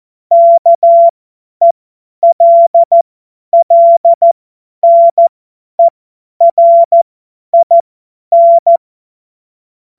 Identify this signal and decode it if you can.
Morse code, heard as a keyed tone, about 14 wpm, spaced as for 10 wpm, 680 Hz. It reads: KELLNERIN
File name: Morsekurs / Witze / Witz 06. April 2021, 50 F70